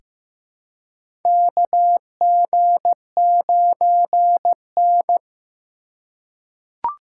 Morse code, heard as a keyed tone, about 15 wpm, 700 Hz. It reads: KG9N
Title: USA Call Signs - 4 Characters - 3x Character Spacing - 15wpm